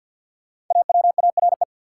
Morse code, heard as a keyed tone, about 47 wpm, 700 Hz. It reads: AWARE
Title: Sentences from Top 1200 Words - New Word Drill - 50wpm